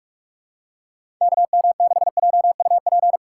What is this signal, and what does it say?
Morse code, heard as a keyed tone, about 47 wpm, 700 Hz. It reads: KM6JUP
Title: USA Call Signs - 6 Characters - 4x Character Spacing - 45wpm